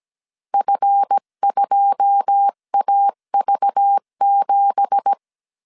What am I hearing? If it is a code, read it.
Morse code, heard as a keyed tone, about 17 wpm, 800 Hz. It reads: F2AV7
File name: CW / DET / 1 9 1500C S B 16wpm 2856